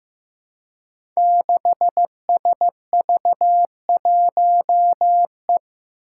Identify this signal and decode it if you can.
Morse code, heard as a keyed tone, about 15 wpm, 700 Hz. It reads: 6SV1E
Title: Five Challenging Letter-Number Combinations 15wpm